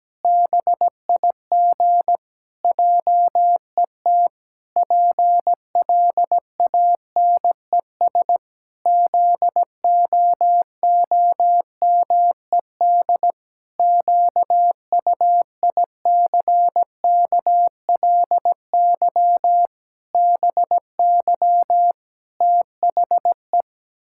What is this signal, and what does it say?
Morse code, heard as a keyed tone, about 17 wpm, 700 Hz. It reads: BIG JET PLANES ZOOMED QUICKLY BY THE
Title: Pangram Challenge v2 - 17wpm